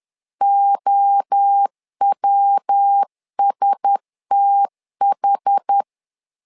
Morse code, heard as a keyed tone, about 10 wpm, 800 Hz. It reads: OWSTH
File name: LEZIONI / 4 Q1S 10wpm 2133